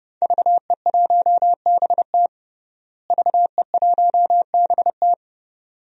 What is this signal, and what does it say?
Morse code, heard as a keyed tone, about 30 wpm, 700 Hz. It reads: VE1BT VE1BT
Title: DXCC-100 Call Signs - 5 Characters - Repeated 2x - 30wpm